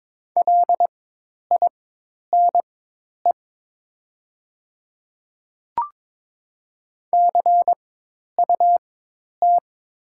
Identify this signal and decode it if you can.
Morse code, heard as a keyed tone, about 22 wpm, spaced as for 10 wpm, 700 Hz. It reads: LINE CUT
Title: 4x Character Spacing - Top 300 Words - 22wpm